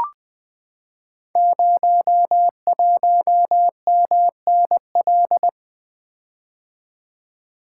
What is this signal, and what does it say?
Morse code, heard as a keyed tone, about 20 wpm, 700 Hz. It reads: 01MNL
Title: Code Groups - 5 Letter:Numbers - 20wpm